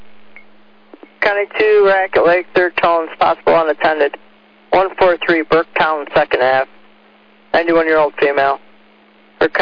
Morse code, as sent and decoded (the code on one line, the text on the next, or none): none